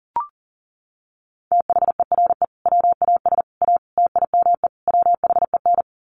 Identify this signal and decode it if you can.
Morse code, heard as a keyed tone, about 40 wpm, 700 Hz. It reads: THERE WAS A TIME WHEN
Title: Sets of 5 Words 40wpm